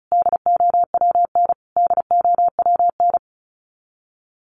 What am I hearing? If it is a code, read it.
Morse code, heard as a keyed tone, about 34 wpm, 700 Hz. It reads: DOWN DOWN
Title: Top 300 Words Repeated 35wpm